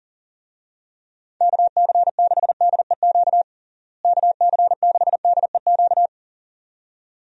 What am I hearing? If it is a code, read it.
Morse code, heard as a keyed tone, about 40 wpm, 700 Hz. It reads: KC6DEQ KC6DEQ
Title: USA Call Signs - 6 Characters - Repeated 2x - 40wpm